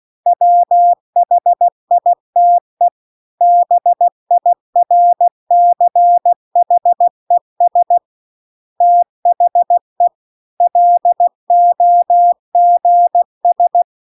Morse code, heard as a keyed tone, about 16 wpm, 700 Hz. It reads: WHITE BIRCHES THE LOGS
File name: CW-The-Big-Twoheaded-River-2-16WPM